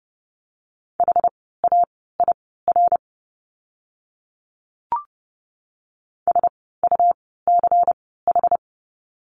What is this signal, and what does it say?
Morse code, heard as a keyed tone, about 30 wpm, spaced as for 17 wpm, 700 Hz. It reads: HAIR SUCH
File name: Lesson 17f - Letter u - Words 30wpm